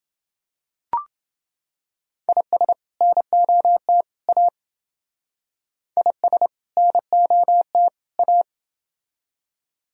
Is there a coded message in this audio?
Yes: Morse code from 2.3 to 8.4 s, heard as a keyed tone, about 27 wpm, 700 Hz. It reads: IS NOT A IS NOT A